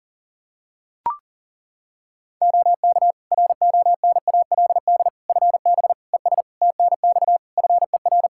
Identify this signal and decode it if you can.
Morse code, heard as a keyed tone, about 40 wpm, 700 Hz. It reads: OK RONALD FB ES TNX FER
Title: Practice QSO - Exchange 2 - Info - 40wpm